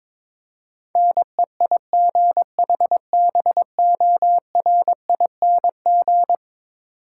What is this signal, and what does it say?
Morse code, heard as a keyed tone, about 22 wpm, 700 Hz. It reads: NEIGHBORING
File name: practice / 22wpm / Eleven Letter Words - 22wpm